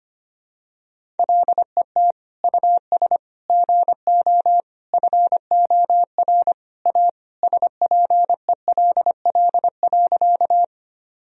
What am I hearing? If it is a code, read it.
Morse code, heard as a keyed tone, about 25 wpm, 700 Hz. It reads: LET US GO FOR A SPELL.
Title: Sentences from Top 1000 Words - Review Part 1 - 25wpm